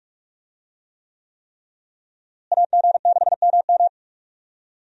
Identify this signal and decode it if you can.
Morse code, heard as a keyed tone, about 46 wpm, 700 Hz. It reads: AG6MM
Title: USA Call Signs - 5 Characters - Repeated 3x - 45wpm